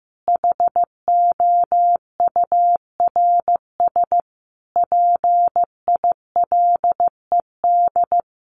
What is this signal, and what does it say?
Morse code, heard as a keyed tone, about 15 wpm, 700 Hz. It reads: HOURS PILED
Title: A Sound Of Thunder 15wpm